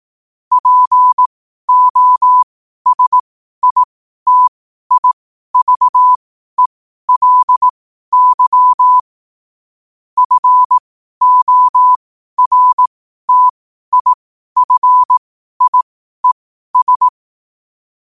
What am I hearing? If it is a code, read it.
Morse code, heard as a keyed tone, about 18 wpm, spaced as for 12 wpm, 1000 Hz. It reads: POSITIVELY FORTIFIES